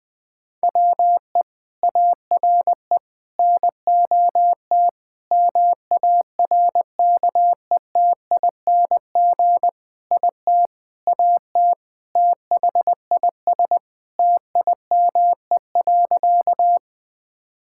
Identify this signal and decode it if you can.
Morse code, heard as a keyed tone, about 20 wpm, 700 Hz. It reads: WE ARE NOT MARKETING IT AT THIS TIME.
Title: Sentences from Top 1000 Words - Review Part 5 - 20wpm